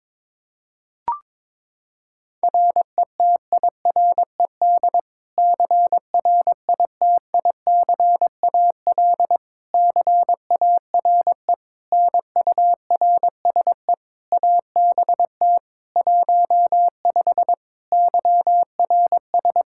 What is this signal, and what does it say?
Morse code, heard as a keyed tone, about 21 wpm, 700 Hz. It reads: RETIRED CRITICAL CARE NURSE ABT 15 YRS
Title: Practice QSO - Exchange 3 - More Info - 22wpm